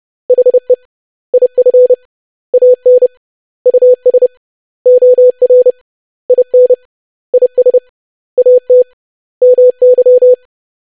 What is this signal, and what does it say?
Morse code, heard as a keyed tone, about 30 wpm, 500 Hz. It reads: HE IF AN US OR IN IS AT MY
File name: lesson / web / basic100-25wpm